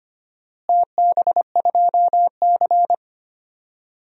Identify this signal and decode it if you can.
Morse code, heard as a keyed tone, about 25 wpm, 700 Hz. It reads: TB2C
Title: DXCC-100 Call Signs - 4 Characters - Repeated 2x - 25wpm